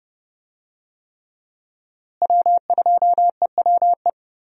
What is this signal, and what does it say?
Morse code, heard as a keyed tone, about 30 wpm, 700 Hz. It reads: W2EWE
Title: USA Call Signs - 5 Characters - Repeated 3x - 30wpm